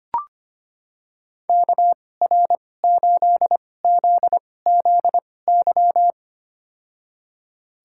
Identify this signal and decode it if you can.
Morse code, heard as a keyed tone, about 25 wpm, spaced as for 18 wpm, 700 Hz. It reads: KR8ZZY
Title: USA Call Signs - 6 Characters - 2x Character Spacing - 25wpm